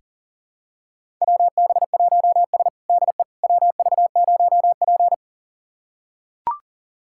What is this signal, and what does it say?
Morse code, heard as a keyed tone, about 40 wpm, 700 Hz. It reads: WB1S DE WV0P
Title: USA Callsign DE Callsign - 4 Characters - 40wpm